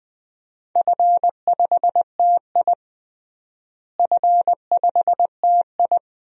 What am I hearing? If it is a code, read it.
Morse code, heard as a keyed tone, about 20 wpm, 700 Hz. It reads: F5TI F5TI
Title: DXCC-100 Call Signs - 4 Characters - Repeated 2x - 20wpm